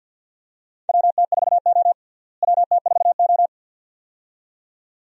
Morse code, heard as a keyed tone, about 47 wpm, 700 Hz. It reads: WT4O WT4O
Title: USA Call Signs - 4 Characters - Repeated 2x - 50wpm